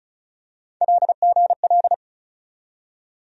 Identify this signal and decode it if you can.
Morse code, heard as a keyed tone, about 36 wpm, 700 Hz. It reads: LGL
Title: USA Call Sign Postfixes - 3 Letters 35wpm